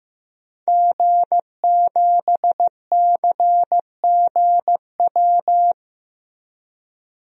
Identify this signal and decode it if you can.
Morse code, heard as a keyed tone, about 15 wpm, 700 Hz. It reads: G7CGW
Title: DXCC-100 Call Signs - 5 Characters - 15wpm